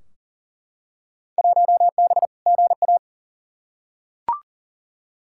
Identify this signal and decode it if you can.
Morse code, heard as a keyed tone, about 40 wpm, 700 Hz. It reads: 1B GA
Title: Field Day Exchanges Encore - 40wpm